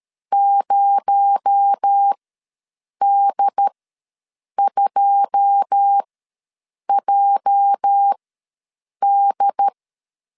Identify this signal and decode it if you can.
Morse code, heard as a keyed tone, about 13 wpm, spaced as for 7 wpm, 800 Hz. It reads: 0D2JD
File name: LEZIONI / 9 2867 ws 12wpm 3431